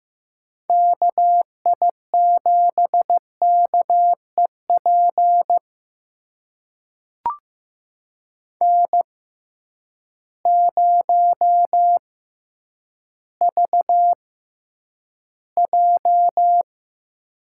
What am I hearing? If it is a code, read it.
Morse code, heard as a keyed tone, about 15 wpm, spaced as for 9 wpm, 700 Hz. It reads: KI7KEP N0VJ
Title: USA Call Signs - 6x Character Spacing - 15wpm